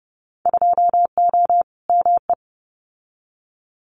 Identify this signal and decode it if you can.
Morse code, heard as a keyed tone, about 30 wpm, 700 Hz. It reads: 2O ME
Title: Winter Field Day Exchanges 30wpm